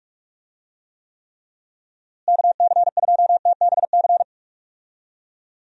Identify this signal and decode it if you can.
Morse code, heard as a keyed tone, about 46 wpm, 700 Hz. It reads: KC2TBC
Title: USA Call Signs - 6 Characters - Repeated 3x - 45wpm